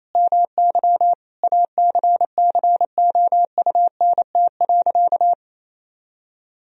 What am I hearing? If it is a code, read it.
Morse code, heard as a keyed tone, about 28 wpm, 700 Hz. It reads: MY ACCOUNT.
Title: Sentences from Top 1000 Words - Review Part 3 - 28wpm